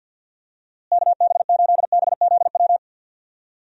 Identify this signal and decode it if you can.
Morse code, heard as a keyed tone, about 50 wpm, 700 Hz. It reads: KB8BZW
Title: USA Call Signs - 6 Characters - Repeated 3x - 50wpm